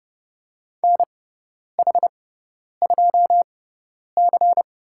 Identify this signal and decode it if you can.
Morse code, heard as a keyed tone, about 30 wpm, spaced as for 10 wpm, 700 Hz. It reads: NH2C
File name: Lesson 16g - Letter c - Call Signs 30wpm